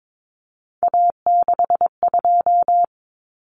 Morse code, heard as a keyed tone, about 22 wpm, 700 Hz. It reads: A62
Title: Common Cut Numbers - 3-Digit 22wpm